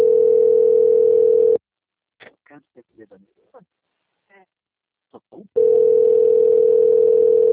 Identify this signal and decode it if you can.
Ringback tone (North America), 440 + 480 Hz, with speech.
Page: Marilia - SP